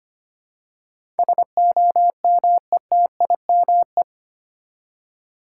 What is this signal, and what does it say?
Morse code, heard as a keyed tone, about 25 wpm, 700 Hz. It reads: SOMETIME
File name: 8 Letter Words - First 3 Letters Given - 25wpm